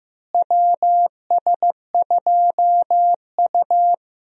Morse code, heard as a keyed tone, about 15 wpm, 700 Hz. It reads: WS2U
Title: USA Call Signs - 4 Characters - Repeated 2x - 15wpm